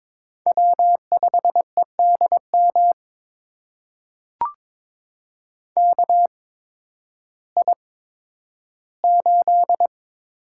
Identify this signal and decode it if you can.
Morse code, heard as a keyed tone, about 22 wpm, spaced as for 12 wpm, 700 Hz. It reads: W5EDM KI8
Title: USA Call Signs - 5 Characters - 8x Character Spacing - 22wpm